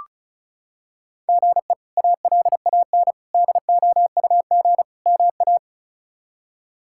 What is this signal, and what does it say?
Morse code, heard as a keyed tone, about 36 wpm, 700 Hz. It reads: GE ALAN DOUG MA
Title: K1USN SST QSO Practice - 35wpm